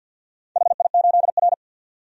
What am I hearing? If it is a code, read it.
Morse code, heard as a keyed tone, about 51 wpm, 700 Hz. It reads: HI8R
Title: DXCC-100 Call Signs - 4 Characters - 50wpm